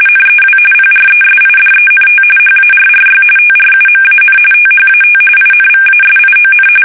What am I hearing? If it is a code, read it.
RTTY, about 45.45 baud. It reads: NEW PRESIDENT HARRY TRUMAN THAT INVASION
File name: fsk850 / battleshipnj